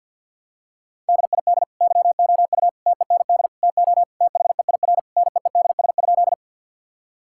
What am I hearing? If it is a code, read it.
Morse code, heard as a keyed tone, about 49 wpm, 700 Hz. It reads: DID YOU TEND TO THEIR NEEDS?
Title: Sentences from Top 1000 Words - Review Part 5 - 50wpm